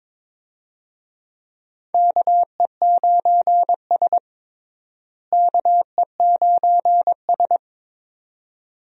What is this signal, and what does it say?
Morse code, heard as a keyed tone, about 22 wpm, 700 Hz. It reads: KE9S KE9S
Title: USA Call Signs - 4 Characters - Repeated 2x - 22wpm